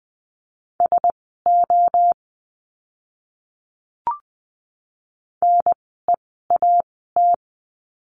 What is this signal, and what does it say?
Morse code, heard as a keyed tone, about 20 wpm, spaced as for 15 wpm, 700 Hz. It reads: SO NEAT
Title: 061 Lesson 11f - Letter h - Words 20wpm